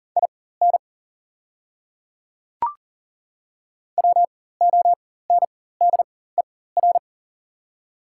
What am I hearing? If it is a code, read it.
Morse code, heard as a keyed tone, about 38 wpm, spaced as for 18 wpm, 700 Hz. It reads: IN WONDER
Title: 4x Character Spacing - Top 500 Words - 40wpm